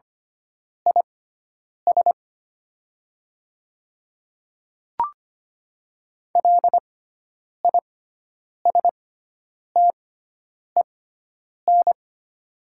Morse code, heard as a keyed tone, about 25 wpm, spaced as for 9 wpm, 690 Hz. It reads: IS LISTEN
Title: 6x Character Spacing - Top 500 Words - 25wpm